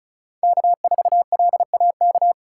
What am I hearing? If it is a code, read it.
Morse code, heard as a keyed tone, about 36 wpm, 700 Hz. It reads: K4LAK
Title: USA Call Signs - 5 Characters - 2x Character Spacing - 35wpm